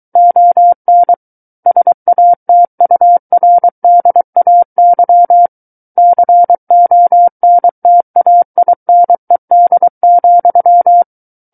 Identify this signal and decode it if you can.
Morse code, heard as a keyed tone, about 23 wpm, 700 Hz. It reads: ON SATURDAY CONTAINED,
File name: Book One / 14-Chapter 14 - 23 WPM